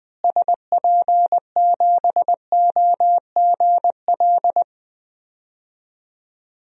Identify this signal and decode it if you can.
Morse code, heard as a keyed tone, about 20 wpm, 700 Hz. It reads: SP7OGL
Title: DXCC-100 Call Signs - 6 Characters - Repeated 2x - 20wpm